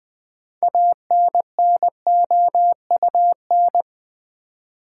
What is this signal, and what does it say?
Morse code, heard as a keyed tone, about 20 wpm, 700 Hz. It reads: ANNOUN
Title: Build A Word - 6 Letters At A Time - 12-Letter Words - 20wpm